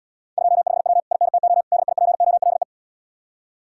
Morse code, heard as a keyed tone, about 69 wpm, 700 Hz. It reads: 25W INTO DIPOLE